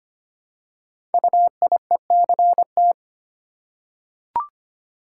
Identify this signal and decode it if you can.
Morse code, heard as a keyed tone, about 25 wpm, 700 Hz. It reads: UIECT
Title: Code Groups - 5 Letters - 25wpm